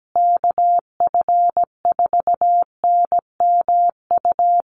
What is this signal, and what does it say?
Morse code, heard as a keyed tone, about 17 wpm, 700 Hz. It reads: KF4NMU
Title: USA Call Signs Repeated 3x 17wpm